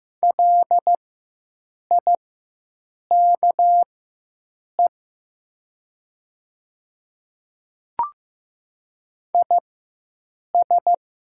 Morse code, heard as a keyed tone, about 15 wpm, spaced as for 7 wpm, 700 Hz. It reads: LIKE IS
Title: 4x Character Spacing - Top 100 Words - 15wpm